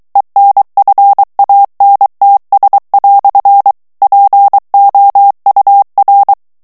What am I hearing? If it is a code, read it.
Morse code, heard as a keyed tone, about 23 wpm, 800 Hz. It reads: ENFANTS" POUR
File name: morse mp3 / FRANCAIS TEXTE CLAIR 1 VITESSE 22 MOTS MINUTES